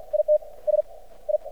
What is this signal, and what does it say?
Morse code, heard as a keyed tone, about 50 wpm, spaced as for 31 wpm, 600 Hz. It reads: ITS I
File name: archived / CRSnet